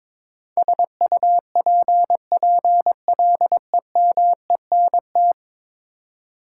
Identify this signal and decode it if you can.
Morse code, heard as a keyed tone, about 21 wpm, 700 Hz. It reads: SUPPLEMENT
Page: morse-code-ninja-practice-22wpm